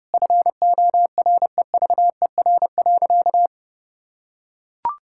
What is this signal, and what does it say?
Morse code, heard as a keyed tone, about 30 wpm, 700 Hz. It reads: FOREVER.
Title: Sentences from Top 1600 Words 30wpm